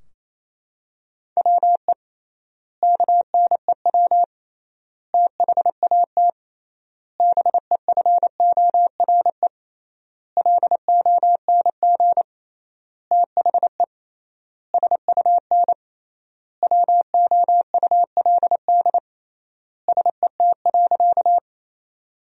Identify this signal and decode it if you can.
Morse code, heard as a keyed tone, about 28 wpm, 700 Hz. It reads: WE KNEW THAT BEFORE LONG THE SUN WOULD SET.
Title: 3x Word-Spacing - Sentences from Top 400 Words - 28wpm